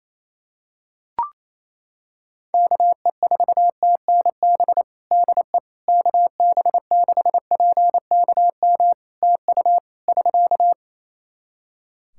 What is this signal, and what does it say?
Morse code, heard as a keyed tone, about 27 wpm, 700 Hz. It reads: KE4TNB DE KB6PKM TU <SK>
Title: Practice QSO - Exchange 4 - Ending - 28wpm